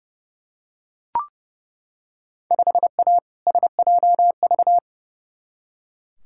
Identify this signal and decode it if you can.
Morse code, heard as a keyed tone, about 30 wpm, 700 Hz. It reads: 5A SJV
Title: Field Day Exchanges - 30wpm